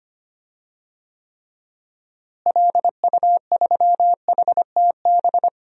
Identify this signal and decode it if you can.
Morse code, heard as a keyed tone, about 25 wpm, 700 Hz. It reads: LU3HTB
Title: DXCC-100 Call Signs - 6 Characters - Repeated 3x - 25wpm